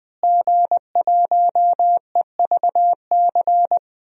Morse code, heard as a keyed tone, about 20 wpm, 700 Hz. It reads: G1EVC
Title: DXCC-100 Call Signs - 5 Characters - Repeated 3x - 20wpm